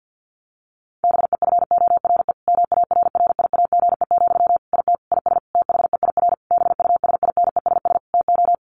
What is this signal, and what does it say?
Morse code, heard as a keyed tone, about 50 wpm, 700 Hz. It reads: BEFORE MARRIAGE, IT IS THEIR BUSINESS TO
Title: Sentences from Top 1500 Words 50wpm